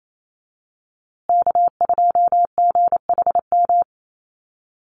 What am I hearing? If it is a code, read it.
Morse code, heard as a keyed tone, about 28 wpm, 700 Hz. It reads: K2GHM
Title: USA Call Signs Repeated 3x 28wpm